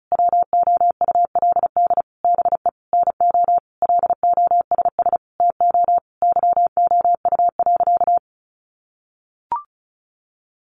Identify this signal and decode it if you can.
Morse code, heard as a keyed tone, about 34 wpm, 700 Hz. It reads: WOULD BE NO LOSS TO YOU.
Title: Sentences from Top 1400 Words 35wpm